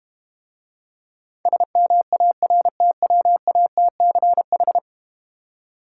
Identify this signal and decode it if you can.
Morse code, heard as a keyed tone, about 32 wpm, 700 Hz. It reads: SMARTWATCH